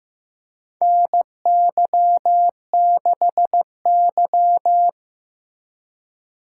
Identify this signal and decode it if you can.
Morse code, heard as a keyed tone, about 15 wpm, 700 Hz. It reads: NY6Y
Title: K1USN SST QSO Practice - 15wpm